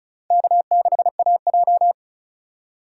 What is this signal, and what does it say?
Morse code, heard as a keyed tone, about 34 wpm, 700 Hz. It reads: K6AJ
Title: USA Call Signs - 4 Characters - 4x Character Spacing - 35wpm